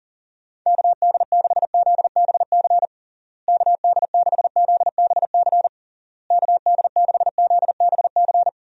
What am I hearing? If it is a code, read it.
Morse code, heard as a keyed tone, about 37 wpm, 700 Hz. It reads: KD6ZBC KD6ZBC KD6ZBC
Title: USA Call Signs - 6 Characters - Repeated 3x - 40wpm